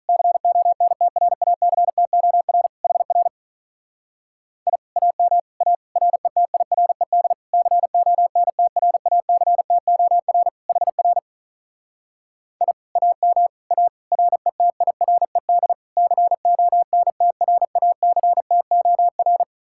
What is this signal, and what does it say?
Morse code, heard as a keyed tone, about 44 wpm, 700 Hz. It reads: CONTRACTOR HR I AM A RETIRED CONTRACTOR HR I AM A RETIRED CONTRACTOR